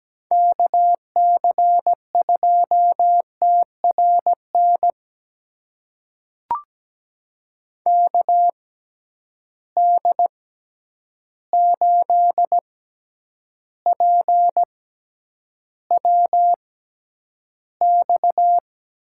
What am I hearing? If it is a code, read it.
Morse code, heard as a keyed tone, about 17 wpm, spaced as for 9 wpm, 700 Hz. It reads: KC2TRN KD8PWX
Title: USA Call Signs - 6 Characters - 6x Character Spacing - 17wpm